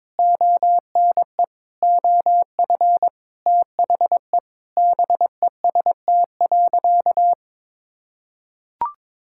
Morse code, heard as a keyed tone, about 22 wpm, 700 Hz. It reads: ONE OF THE BEST.
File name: practice / 22wpm / Sentences with 4-Word Phrases - 22wpm